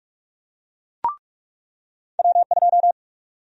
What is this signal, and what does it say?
Morse code, heard as a keyed tone, about 41 wpm, 700 Hz. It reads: W2
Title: USA Call Sign Prefixes - 45wpm